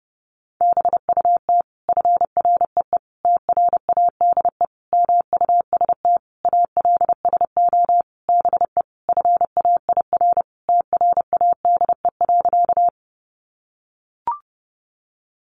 Morse code, heard as a keyed tone, about 30 wpm, 700 Hz. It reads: BUT FREE TRADE MUST ALSO BE FAIR TRADE.
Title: Sentences from Top 1300 Words 30wpm